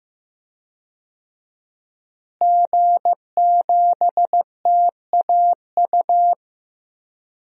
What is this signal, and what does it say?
Morse code, heard as a keyed tone, about 15 wpm, 700 Hz. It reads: G7TAU
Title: DXCC-100 Call Signs - 5 Characters - Repeated 2x - 15wpm